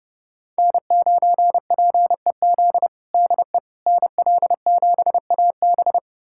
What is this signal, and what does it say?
Morse code, heard as a keyed tone, about 32 wpm, 700 Hz. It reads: N9PEZ DE NL7AB
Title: USA Callsign DE Callsign - 5 Characters - 30wpm